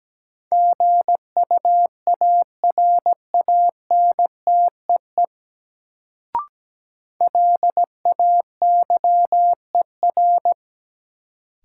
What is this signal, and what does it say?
Morse code, heard as a keyed tone, about 17 wpm, 700 Hz. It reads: GUARANTEE LAYER
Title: Sentences from Top 1600 Words - New Word Drill Alt - 17wpm